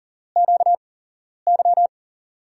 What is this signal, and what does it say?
Morse code, heard as a keyed tone, about 40 wpm, spaced as for 11 wpm, 700 Hz. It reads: QY
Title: USA Call Signs - 8x Character Spacing - 40wpm